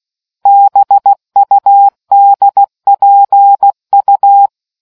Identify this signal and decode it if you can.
Morse code, heard as a keyed tone, about 16 wpm, 800 Hz. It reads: BUDPU